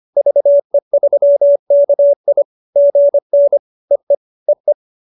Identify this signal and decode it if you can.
Morse code, heard as a keyed tone, about 25 wpm, 570 Hz. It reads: VE3KI GN EE EE